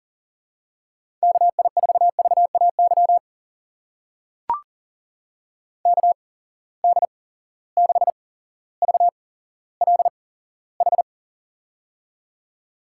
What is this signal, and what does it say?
Morse code, heard as a keyed tone, about 41 wpm, spaced as for 17 wpm, 700 Hz. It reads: KI4VAY KD6VLH
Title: USA Call Signs - 6 Characters - 8x Character Spacing - 40wpm